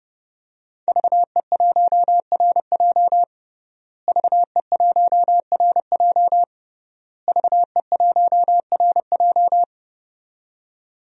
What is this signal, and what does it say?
Morse code, heard as a keyed tone, about 30 wpm, 700 Hz. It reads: VE1RJ VE1RJ VE1RJ
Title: DXCC-100 Call Signs - 5 Characters - Repeated 3x - 30wpm